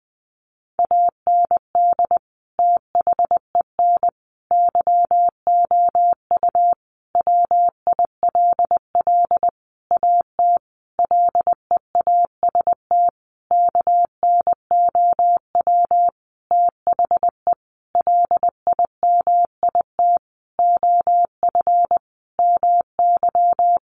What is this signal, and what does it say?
Morse code, heard as a keyed tone, about 20 wpm, 700 Hz. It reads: AND THEN YOU WILL AT LEAST KNOW THE LIMIT OF MY
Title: Sentences from Top 1500 Words 20wpm